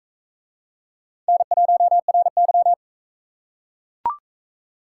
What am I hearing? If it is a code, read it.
Morse code, heard as a keyed tone, about 38 wpm, 700 Hz. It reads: N1RY